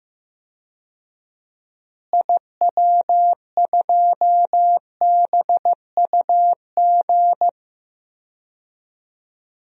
Morse code, heard as a keyed tone, about 15 wpm, 700 Hz. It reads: IW2BUG
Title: DXCC-100 Call Signs - 6 Characters - Repeated 2x - 15wpm